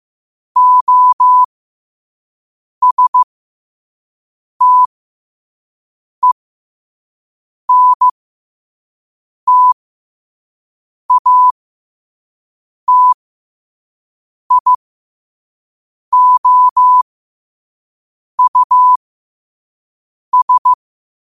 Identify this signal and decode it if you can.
Morse code, heard as a keyed tone, about 15 wpm, spaced as for 5 wpm, 1000 Hz. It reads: OSTENTATIOUS
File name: qotdfiles / 5wpm